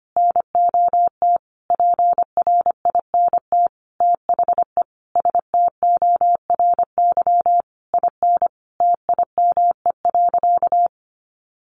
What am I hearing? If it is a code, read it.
Morse code, heard as a keyed tone, about 25 wpm, 700 Hz. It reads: NOT PRINT THE STORY IN TIME.
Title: Sentences from Top 1400 Words 25wpm